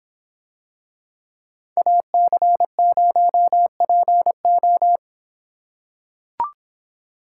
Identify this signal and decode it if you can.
Morse code, heard as a keyed tone, about 25 wpm, 700 Hz. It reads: AC0PO